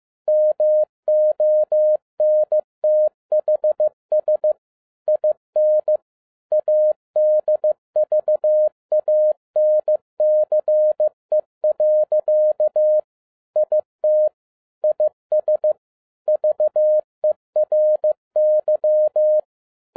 Morse code, heard as a keyed tone, about 15 wpm, 610 Hz. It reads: MONTHS IN ADVANCE. IT IS VERY
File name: 2011-07-mp3 / n12 T8XO 15wpm